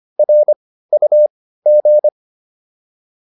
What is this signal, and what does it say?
Morse code, heard as a keyed tone, about 25 wpm, spaced as for 15 wpm, 600 Hz. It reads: RUG